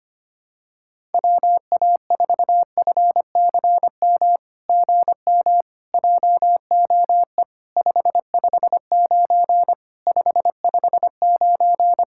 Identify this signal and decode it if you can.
Morse code, heard as a keyed tone, about 25 wpm, 700 Hz. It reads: WA4FCM GM JOE 559 559
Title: SOTA QSO Practice 25wpm